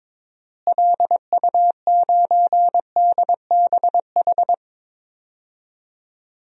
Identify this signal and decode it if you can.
Morse code, heard as a keyed tone, about 22 wpm, 700 Hz. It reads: LU9DBH
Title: DXCC-100 Call Signs - 6 Characters - Repeated 2x - 22wpm